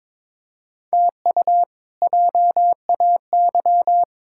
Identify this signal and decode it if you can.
Morse code, heard as a keyed tone, about 22 wpm, 700 Hz. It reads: TU JAY
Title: K1USN SST QSO Practice - 22wpm